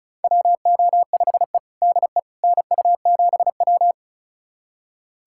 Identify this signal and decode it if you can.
Morse code, heard as a keyed tone, about 36 wpm, 700 Hz. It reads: WO5E DE NU7W
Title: USA Callsign DE Callsign - 4 Characters - 35wpm